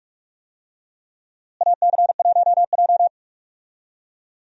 Morse code, heard as a keyed tone, about 46 wpm, 700 Hz. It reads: AC1J
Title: USA Call Signs - 4 Characters - Repeated 3x - 45wpm